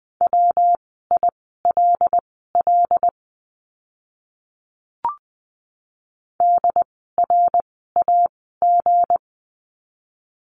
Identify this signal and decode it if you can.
Morse code, heard as a keyed tone, about 20 wpm, spaced as for 15 wpm, 700 Hz. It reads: WILL DRAG
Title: 160 Lesson 26f - Letter g - Words 20wpm